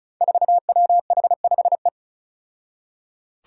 Morse code, heard as a keyed tone, about 34 wpm, 700 Hz. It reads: 4WH5E
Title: Five Challenging Letter-Number Combinations 35wpm